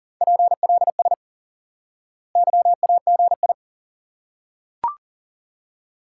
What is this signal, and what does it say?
Morse code, heard as a keyed tone, about 40 wpm, 700 Hz. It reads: PLS YAGI
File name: IQE Course 3 - Lesson 89 - YAGI - Two Elements - 40wpm